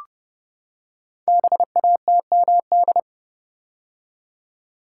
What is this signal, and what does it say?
Morse code, heard as a keyed tone, about 30 wpm, 700 Hz. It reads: BATMD
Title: Code Groups - 5 Letters - 30wpm